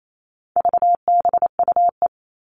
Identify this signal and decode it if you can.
Morse code, heard as a keyed tone, about 28 wpm, 700 Hz. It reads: VBUE
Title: All Cut Numbers - 4-Digit 28wpm